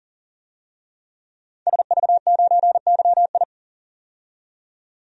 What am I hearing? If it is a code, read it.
Morse code, heard as a keyed tone, about 40 wpm, 700 Hz. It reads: SV9YI
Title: DXCC-100 Call Signs - 5 Characters - 40wpm